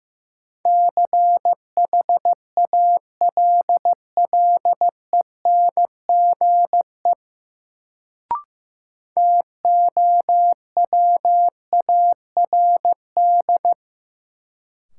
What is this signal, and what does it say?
Morse code, heard as a keyed tone, about 15 wpm, 700 Hz. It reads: CHALLENGE TOWARD
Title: Sentences from Top 1300 Words - New Word Drill Alt - 15wpm